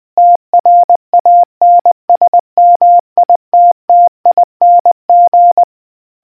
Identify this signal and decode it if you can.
Morse code, heard as a keyed tone, about 20 wpm, 700 Hz. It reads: TRANSMITTING